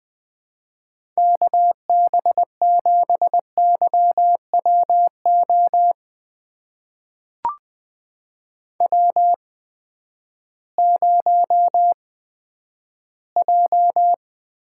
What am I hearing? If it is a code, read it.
Morse code, heard as a keyed tone, about 20 wpm, spaced as for 11 wpm, 700 Hz. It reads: KB7YWO W0J
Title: USA Call Signs - 8x Character Spacing - 20wpm